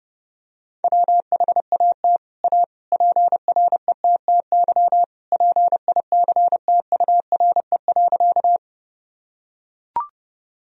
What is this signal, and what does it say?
Morse code, heard as a keyed tone, about 32 wpm, 700 Hz. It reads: WHAT A PRETTY PICTURE.
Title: Sentences from Top 1000 Words - Review Part 1 - 30wpm